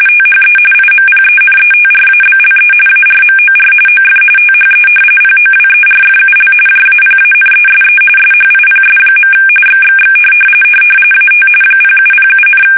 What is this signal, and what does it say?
RTTY, about 45.45 baud. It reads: THE IMPENETRABLE ARMOR OF THE POWERFUL JAPANESE SHIPS. XXX AT 0707,
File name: fsk850 / battleshipnj